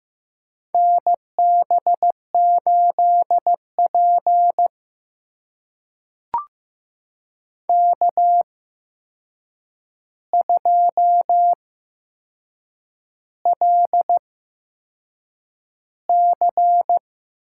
Morse code, heard as a keyed tone, about 15 wpm, spaced as for 6 wpm, 700 Hz. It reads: NB8P K2LC
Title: USA Call Signs - 4 Characters - 8x Character Spacing - 15wpm